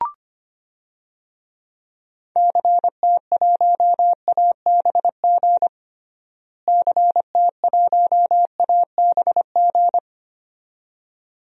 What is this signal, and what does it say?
Morse code, heard as a keyed tone, about 25 wpm, 700 Hz. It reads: CT1ABG CT1ABG
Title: DXCC-100 Call Signs - 6 Characters - Repeated 2x - 25wpm